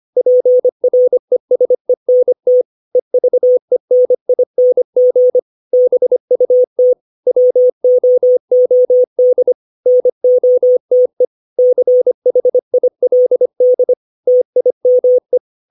Morse code, heard as a keyed tone, about 25 wpm, 500 Hz. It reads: PRESENT EVENING BUT WOOD NOTE CHILD TIME
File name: 632 words / 632 Words 25WPM